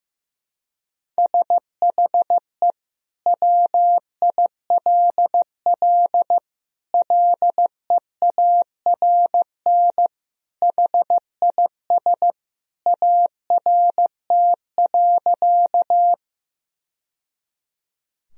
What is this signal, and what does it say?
Morse code, heard as a keyed tone, about 15 wpm, 700 Hz. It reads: SHE WILL LEARN HIS ART.
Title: Sentences from Top 1000 Words - Review Part 3 - 15wpm